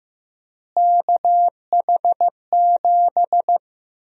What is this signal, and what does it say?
Morse code, heard as a keyed tone, about 15 wpm, 700 Hz. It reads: KH7
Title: USA Call Sign Prefixes - Repeated 2x - 15wpm